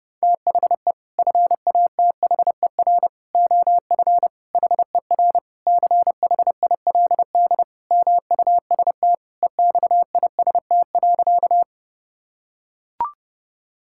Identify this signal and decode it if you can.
Morse code, heard as a keyed tone, about 30 wpm, 700 Hz. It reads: THE FATHER OF HER CHILD MUST EXIST.
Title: Sentences from Top 1000 Words - Review Part 5 - 30wpm